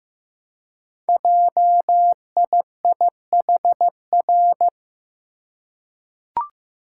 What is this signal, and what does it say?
Morse code, heard as a keyed tone, about 15 wpm, 700 Hz. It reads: JIIHR
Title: Code Groups - 5 Letters - 15wpm